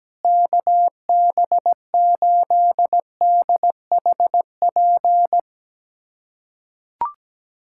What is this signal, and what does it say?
Morse code, heard as a keyed tone, about 17 wpm, 700 Hz. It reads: KB8DHP
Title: USA Callsign DE Callsign 17wpm